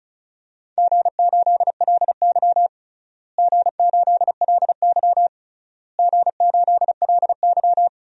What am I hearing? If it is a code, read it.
Morse code, heard as a keyed tone, about 36 wpm, 690 Hz. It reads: G8LY G8LY G8LY
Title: DXCC-100 Call Signs - 4 Characters - Repeated 3x - 35wpm